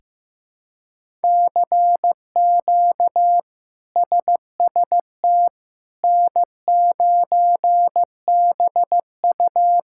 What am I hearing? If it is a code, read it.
Morse code, heard as a keyed tone, about 15 wpm, 700 Hz. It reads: CQ SST N9BU
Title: K1USN SST QSO Practice - 15wpm